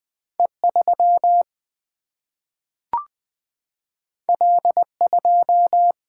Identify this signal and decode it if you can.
Morse code, heard as a keyed tone, about 20 wpm, 700 Hz. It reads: E3 L2
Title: Single-Letter Single-Number 20wpm